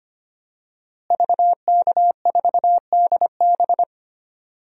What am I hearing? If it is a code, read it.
Morse code, heard as a keyed tone, about 25 wpm, 700 Hz. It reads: VK4DB
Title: DXCC-100 Call Signs - 5 Characters - Repeated 3x - 25wpm